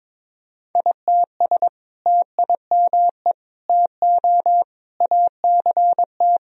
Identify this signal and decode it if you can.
Morse code, heard as a keyed tone, about 22 wpm, 700 Hz. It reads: ITS TIME TO ACT
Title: Sentences from Top 1000 Words - Review Part 1 - 22wpm